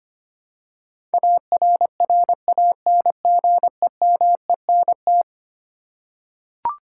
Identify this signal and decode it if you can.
Morse code, heard as a keyed tone, about 25 wpm, 700 Hz. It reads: ARRANGEMENT
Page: morse-code-ninja-practice-25wpm